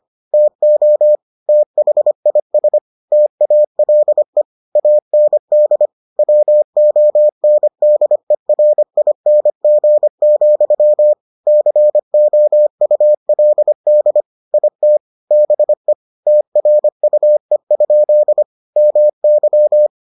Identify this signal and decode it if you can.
Morse code, heard as a keyed tone, about 25 wpm, 600 Hz. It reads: TO THIS TALE AND WONDERING, COULD IT BE TRUE? MY